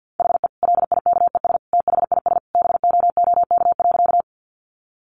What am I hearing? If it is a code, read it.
Morse code, heard as a keyed tone, about 50 wpm, 700 Hz. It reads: HE LIKES THIS BOOK.
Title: Additional Easy Phrases 50wpm